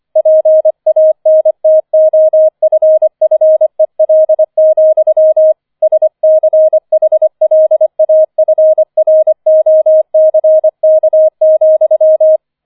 Morse code, heard as a keyed tone, about 25 wpm, 630 Hz. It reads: PANTOFFEL, SCHLAFROCK,
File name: 005 Fünfter Streich 25wpm